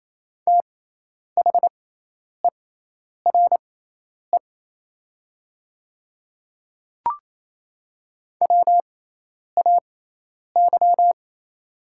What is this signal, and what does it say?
Morse code, heard as a keyed tone, about 28 wpm, spaced as for 10 wpm, 700 Hz. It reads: THERE WAY
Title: 6x Character Spacing - Top 100 Words - 28wpm